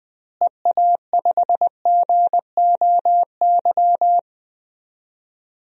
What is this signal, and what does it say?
Morse code, heard as a keyed tone, about 20 wpm, 700 Hz. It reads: EA5GOY
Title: DXCC-100 Call Signs - 6 Characters - Repeated 3x - 20wpm